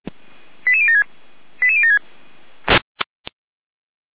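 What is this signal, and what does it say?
Five-tone selective call (ZVEI), heard as quick runs of five tones, tones of about 70 ms. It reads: 88076 88076
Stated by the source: ZVEI five-tone selective call received by SDR.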